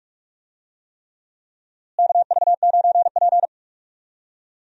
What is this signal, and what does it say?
Morse code, heard as a keyed tone, about 44 wpm, 700 Hz. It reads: KV9P
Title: USA Call Signs - 4 Characters - Repeated 3x - 45wpm